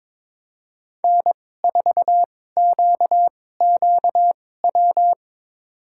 Morse code, heard as a keyed tone, about 22 wpm, spaced as for 16 wpm, 700 Hz. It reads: N4QQW
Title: USA Call Signs - 5 Characters - 2x Character Spacing - 22wpm